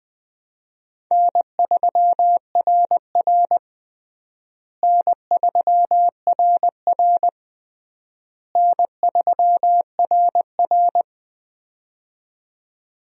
Morse code, heard as a keyed tone, about 20 wpm, 700 Hz. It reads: N3RR N3RR N3RR
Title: USA Call Signs - 4 Characters - Repeated 3x - 20wpm